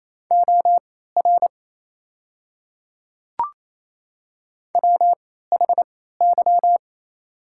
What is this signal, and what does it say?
Morse code, heard as a keyed tone, about 28 wpm, spaced as for 16 wpm, 700 Hz. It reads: OR WHY
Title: 3x Character Spacing - Top 300 Words - 28wpm